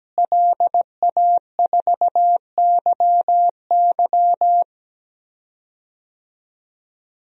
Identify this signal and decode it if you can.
Morse code, heard as a keyed tone, about 17 wpm, 700 Hz. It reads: LA4YY
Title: DXCC-100 Call Signs - 5 Characters - Repeated 2x - 17wpm